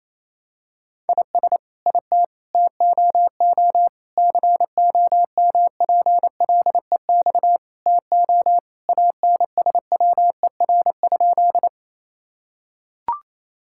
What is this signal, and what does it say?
Morse code, heard as a keyed tone, about 28 wpm, 700 Hz. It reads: IS IT TOO COMPLEX TO ANSWER?
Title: Sentences from Top 1000 Words - Review Part 6 - 28wpm